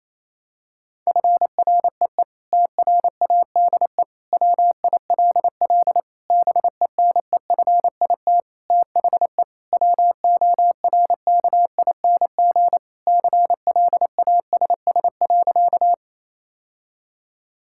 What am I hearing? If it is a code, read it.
Morse code, heard as a keyed tone, about 28 wpm, 700 Hz. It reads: FREE TRADE WILL BENEFIT THE WORKING CLASS.
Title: Sentences from Top 1000 Words - Review Part 6 - 28wpm